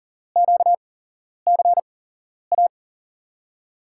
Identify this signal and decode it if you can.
Morse code, heard as a keyed tone, about 40 wpm, spaced as for 11 wpm, 700 Hz. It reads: QCA
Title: USA Call Signs - 6 Characters - 8x Character Spacing - 40wpm